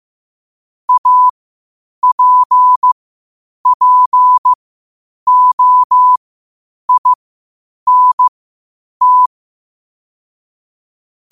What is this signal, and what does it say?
Morse code, heard as a keyed tone, about 15 wpm, spaced as for 8 wpm, 1000 Hz. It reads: APPOINT
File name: qotdfiles / 8wpm